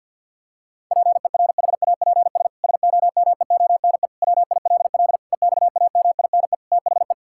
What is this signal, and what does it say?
Morse code, heard as a keyed tone, about 50 wpm, 700 Hz. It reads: PERHAPS SOMEONE WILL EXAMINE THE
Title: Sentences from Top 1600 Words 50wpm